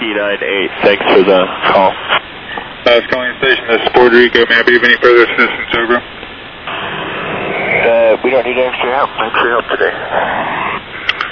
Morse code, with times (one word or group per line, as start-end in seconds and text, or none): none